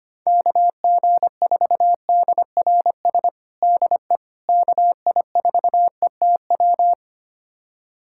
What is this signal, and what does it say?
Morse code, heard as a keyed tone, about 25 wpm, 700 Hz. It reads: KG4DRS DE KI4ETW
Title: USA Callsign DE Callsign - 6 Characters - 25wpm